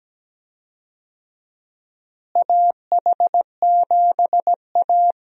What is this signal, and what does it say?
Morse code, heard as a keyed tone, about 17 wpm, 700 Hz. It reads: AH7A
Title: USA Call Signs - 4 Characters - Repeated 2x - 17wpm